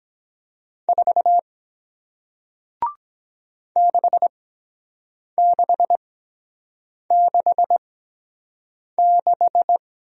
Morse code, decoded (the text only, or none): 4 6666